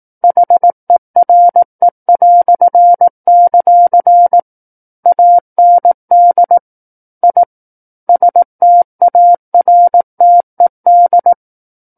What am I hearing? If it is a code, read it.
Morse code, heard as a keyed tone, about 18 wpm, 700 Hz. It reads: HERE"; AND I STARTED
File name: 18 WPM